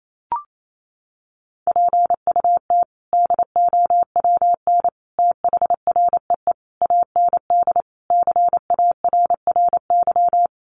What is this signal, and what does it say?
Morse code, heard as a keyed tone, about 28 wpm, 700 Hz. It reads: PUT DOWN THREE AND CARRY
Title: Sentences from Top 400 Words 28wpm